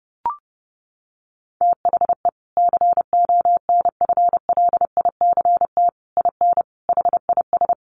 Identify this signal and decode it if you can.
Morse code, heard as a keyed tone, about 30 wpm, 700 Hz. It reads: THE CONFLICT IN HIS
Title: Sentences from Top 1300 Words 30wpm